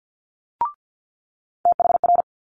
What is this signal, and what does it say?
Morse code, heard as a keyed tone, about 50 wpm, 700 Hz. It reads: THR
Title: Multi-letter Phonemes Quick Fire 50wpm